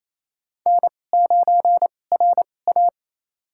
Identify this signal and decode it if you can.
Morse code, heard as a keyed tone, about 27 wpm, spaced as for 20 wpm, 700 Hz. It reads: N9RA
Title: USA Call Signs - 4 Characters - 2x Character Spacing - 28wpm